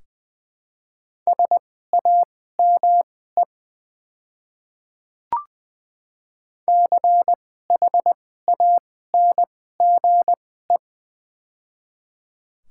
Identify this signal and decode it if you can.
Morse code, heard as a keyed tone, about 21 wpm, spaced as for 15 wpm, 700 Hz. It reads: SAME CHANGE
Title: 2x Character Spacing - Top 200 Words - 20wpm